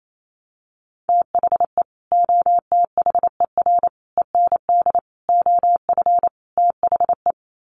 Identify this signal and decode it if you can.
Morse code, heard as a keyed tone, about 28 wpm, 700 Hz. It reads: THE OTHER END OF THE
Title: Sets of 5 Words 28wpm